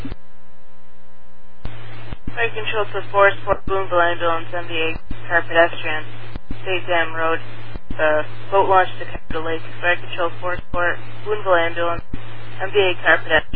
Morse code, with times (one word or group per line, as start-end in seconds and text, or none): none